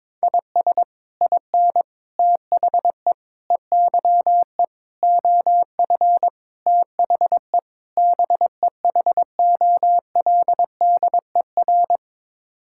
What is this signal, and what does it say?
Morse code, heard as a keyed tone, about 22 wpm, 700 Hz. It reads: IS IN THE EYE OF THE BEHOLDER
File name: practice / 22wpm / Sets of 7 Words - 22wpm